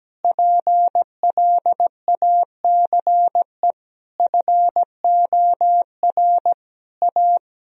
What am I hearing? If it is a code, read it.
Morse code, heard as a keyed tone, about 17 wpm, 700 Hz. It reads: PLACE FOR A
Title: Sentences from Top 1000 Words - Review Part 3 - 17wpm